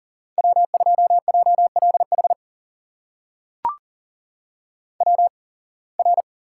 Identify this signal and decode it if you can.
Morse code, heard as a keyed tone, about 40 wpm, spaced as for 26 wpm, 700 Hz. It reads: W2JLH WR
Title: USA Call Signs - 5 Characters - 8x Character Spacing - 40wpm